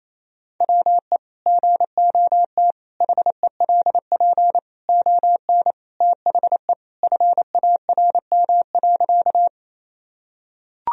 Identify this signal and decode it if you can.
Morse code, heard as a keyed tone, about 28 wpm, 690 Hz. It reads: WE GOT HELP ON THE FARM.
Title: Sentences from Top 1000 Words - Review Part 2 - 28wpm